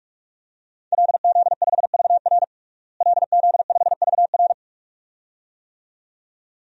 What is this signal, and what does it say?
Morse code, heard as a keyed tone, about 43 wpm, 700 Hz. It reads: LZ5VR LZ5VR
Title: DXCC-100 Call Signs - 5 Characters - Repeated 2x - 45wpm